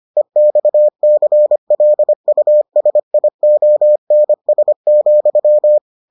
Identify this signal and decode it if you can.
Morse code, heard as a keyed tone, about 25 wpm, 600 Hz. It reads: EXCLUSIONS,